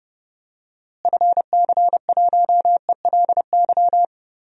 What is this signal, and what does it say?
Morse code, heard as a keyed tone, about 30 wpm, 700 Hz. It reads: FC1ELY
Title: DXCC-100 Call Signs - 6 Characters - Repeated 2x - 30wpm